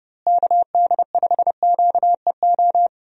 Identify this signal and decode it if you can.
Morse code, heard as a keyed tone, about 32 wpm, 700 Hz. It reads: KD5QEO
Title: USA Call Signs - 6 Characters - Repeated 3x - 30wpm